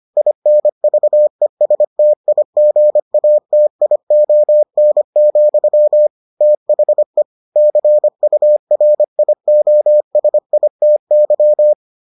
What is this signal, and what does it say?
Morse code, heard as a keyed tone, about 25 wpm, 600 Hz. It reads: INVESTIGATION, THE CURIOSITY